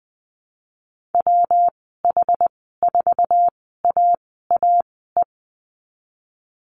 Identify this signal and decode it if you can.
Morse code, heard as a keyed tone, about 20 wpm, spaced as for 15 wpm, 700 Hz. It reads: WH4AAE
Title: 238 Lesson 38g - Letter q - Call Signs 20wpm